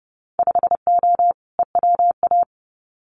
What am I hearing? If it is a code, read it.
Morse code, heard as a keyed tone, about 30 wpm, 700 Hz. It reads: 5O EWA
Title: Winter Field Day Exchanges Encore 30wpm